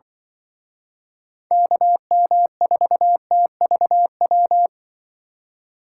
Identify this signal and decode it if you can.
Morse code, heard as a keyed tone, about 25 wpm, 700 Hz. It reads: KM4TVW
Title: Speed Racing - USA Call Signs 20wpm